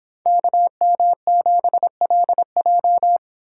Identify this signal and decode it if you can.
Morse code, heard as a keyed tone, about 24 wpm, 700 Hz. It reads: KM7LJ